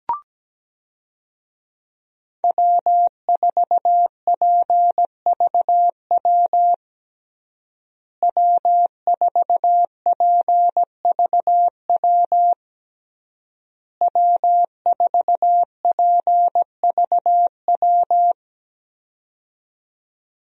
Morse code, heard as a keyed tone, about 17 wpm, 700 Hz. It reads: W4PVW W4PVW W4PVW
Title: USA Call Signs - 5 Characters - Repeated 3x - 17wpm